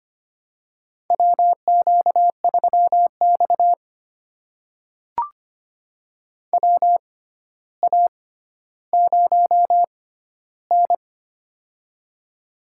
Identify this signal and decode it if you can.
Morse code, heard as a keyed tone, about 25 wpm, spaced as for 13 wpm, 700 Hz. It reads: WQ3X WA0N
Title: USA Call Signs - 4 Characters - 6x Character Spacing - 25wpm